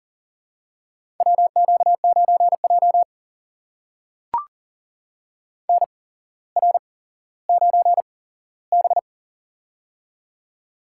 Morse code, heard as a keyed tone, about 42 wpm, spaced as for 17 wpm, 700 Hz. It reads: WQ9J NR9B
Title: USA Call Signs - 4 Characters - 8x Character Spacing - 40wpm